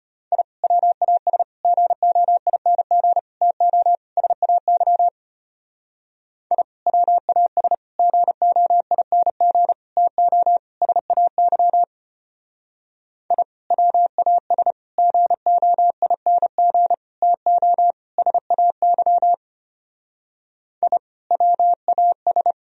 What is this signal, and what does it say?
Morse code, heard as a keyed tone, about 32 wpm, 700 Hz. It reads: I WAS GOING TO SAY I WAS GOING TO SAY I WAS GOING TO SAY I WAS